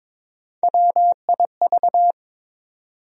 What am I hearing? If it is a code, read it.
Morse code, heard as a keyed tone, about 22 wpm, 700 Hz. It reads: WIV
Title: USA Call Sign Postfixes - 3 Letters 22wpm